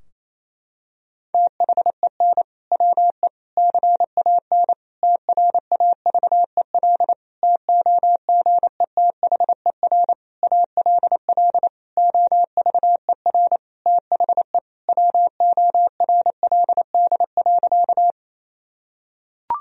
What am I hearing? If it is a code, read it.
Morse code, heard as a keyed tone, about 28 wpm, 700 Hz. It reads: THEN WE CAN TRAVEL TOGETHER ALL OVER THE WORLD.
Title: Sentences from Top 1000 Words - Review Part 2 - 28wpm